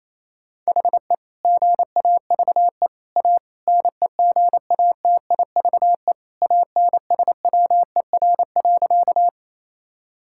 Morse code, heard as a keyed tone, about 28 wpm, 700 Hz. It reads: HE GAVE A NEGATIVE ANSWER.
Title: Sentences from Top 1000 Words - Review Part 6 - 28wpm